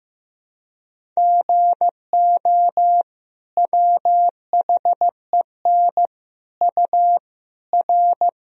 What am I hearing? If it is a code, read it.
Morse code, heard as a keyed tone, about 15 wpm, 700 Hz. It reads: GO WHEN U R
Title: Sentences from Top 500 Words and Abbreviations 15wpm